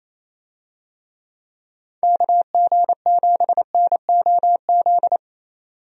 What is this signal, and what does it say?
Morse code, heard as a keyed tone, about 28 wpm, 700 Hz. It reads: KG7NOZ
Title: USA Call Signs - 6 Characters - 28wpm